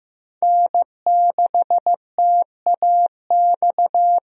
Morse code, heard as a keyed tone, about 15 wpm, 700 Hz. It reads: N6TAX
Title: USA Call Signs - 5 Characters - Repeated 3x - 15wpm